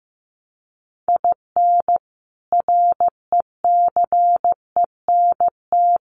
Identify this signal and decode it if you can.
Morse code, heard as a keyed tone, about 15 wpm, 700 Hz. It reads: IN RECENT
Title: Sets of 3 Words 15wpm